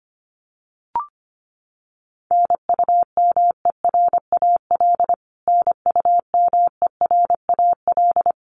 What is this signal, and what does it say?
Morse code, heard as a keyed tone, about 25 wpm, 700 Hz. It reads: NUMERAL NUMERAL
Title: Top 500 Words Repeated 25wpm